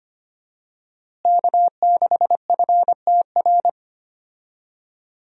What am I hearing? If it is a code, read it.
Morse code, heard as a keyed tone, about 25 wpm, 700 Hz. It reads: K6FTR
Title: USA Call Signs - 5 Characters - Repeated 3x - 25wpm